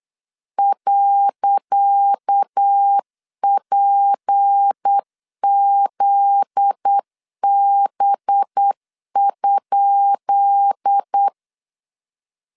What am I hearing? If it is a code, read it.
Morse code, heard as a keyed tone, about 8 wpm, 800 Hz. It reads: .PZB?
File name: LEZIONI / 10 SIGNS / 10C DOT 8wpm 3649